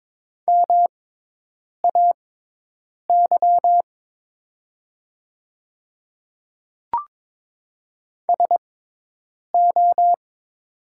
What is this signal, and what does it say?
Morse code, heard as a keyed tone, about 22 wpm, spaced as for 8 wpm, 700 Hz. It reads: MAY SO
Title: 6x Character Spacing - Top 100 Words - 22wpm